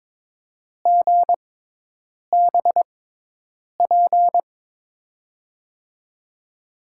Morse code, heard as a keyed tone, about 22 wpm, spaced as for 8 wpm, 700 Hz. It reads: GBP
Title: USA Call Signs - 6x Character Spacing - 22wpm